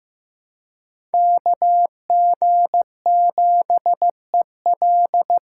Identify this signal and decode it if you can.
Morse code, heard as a keyed tone, about 18 wpm, 700 Hz. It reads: KG7EL
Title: Speed Racing - USA Call Signs 15wpm